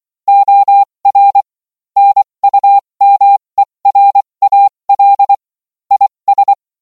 Morse code, heard as a keyed tone, about 24 wpm, spaced as for 20 wpm, 800 Hz. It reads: OR NUMERAL IS